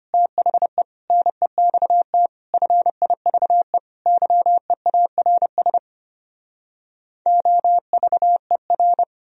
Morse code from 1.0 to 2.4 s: NEXT

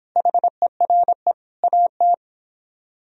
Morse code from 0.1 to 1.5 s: HERE

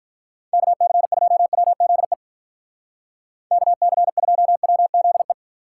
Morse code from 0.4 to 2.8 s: KC2WZE

